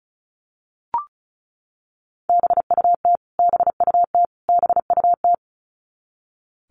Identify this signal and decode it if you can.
Morse code, heard as a keyed tone, about 34 wpm, 700 Hz. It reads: BUT BUT BUT
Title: Top 200 Words Repeated 35wpm